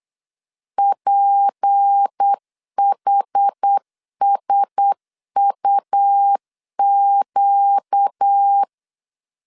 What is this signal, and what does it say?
Morse code, heard as a keyed tone, about 8 wpm, 800 Hz. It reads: PHSUQ
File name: LEZIONI / 6 PKX0 / 6 PK 8wpm 3003